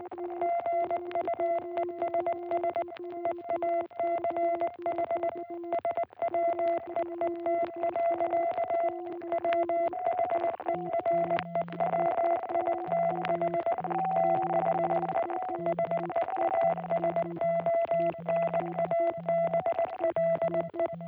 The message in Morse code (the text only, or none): GUESS EACH SWEET. WHEREWITH THE SEASONABLE